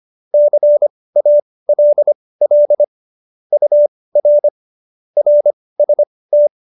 Morse code, heard as a keyed tone, about 25 wpm, spaced as for 18 wpm, 590 Hz. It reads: CALL UR RST
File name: qso208b 18